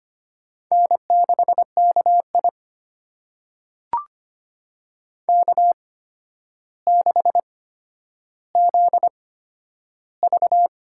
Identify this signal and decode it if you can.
Morse code, heard as a keyed tone, about 25 wpm, spaced as for 11 wpm, 700 Hz. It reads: N6KI K6ZV
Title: USA Call Signs - 4 Characters - 8x Character Spacing - 25wpm